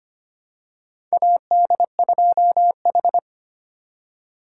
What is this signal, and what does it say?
Morse code, heard as a keyed tone, about 25 wpm, 700 Hz. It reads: AD2H
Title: USA Call Signs - 4 Characters - 2x Character Spacing - 25wpm